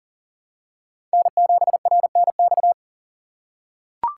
Morse code, heard as a keyed tone, about 40 wpm, 700 Hz. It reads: N7RNX
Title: USA Call Signs - 3x Character Spacing - 40wpm